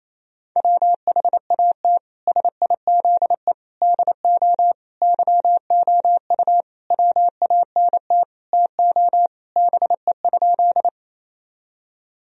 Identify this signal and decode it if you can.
Morse code, heard as a keyed tone, about 28 wpm, 700 Hz. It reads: WHAT SIZE DO YOU WANT TO BE?
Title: Sentences from Top 1000 Words - Review Part 2 - 28wpm